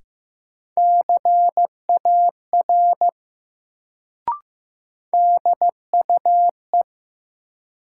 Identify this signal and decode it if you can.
Morse code, heard as a keyed tone, about 15 wpm, 700 Hz. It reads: CAR DUE
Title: Sets of 5 Words Encore - New Word Drill - 15wpm